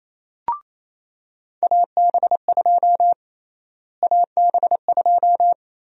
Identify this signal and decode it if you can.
Morse code, heard as a keyed tone, about 28 wpm, 700 Hz. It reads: AB2 AB2
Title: USA Call Sign Prefixes - Repeated 2x - 28wpm